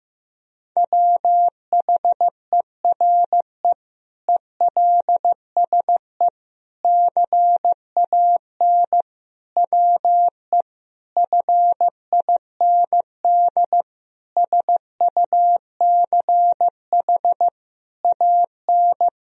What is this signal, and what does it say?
Morse code, heard as a keyed tone, about 15 wpm, 700 Hz. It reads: WHERE ELSE CAN WE FIND SUCH AN
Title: Sentences from Top 1000 Words - Review Part 1 - 15wpm